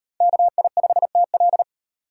Morse code, heard as a keyed tone, about 38 wpm, 700 Hz. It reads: KI5TL